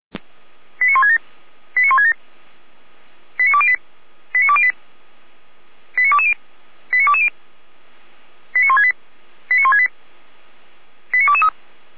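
Five-tone selective call (ZVEI), heard as quick runs of five tones, tones of about 70 ms. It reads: 89167 89167 89208 89208 89220 89220 89168 89168 89223
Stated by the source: ZVEI five-tone selective call received via SDR.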